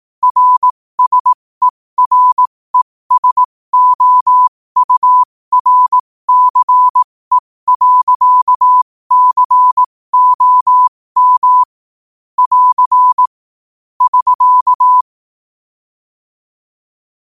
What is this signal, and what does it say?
Morse code, heard as a keyed tone, about 18 wpm, spaced as for 15 wpm, 1000 Hz. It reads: RSERESOURCE.COM <AR> <SK>
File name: qotdfiles / 15wpm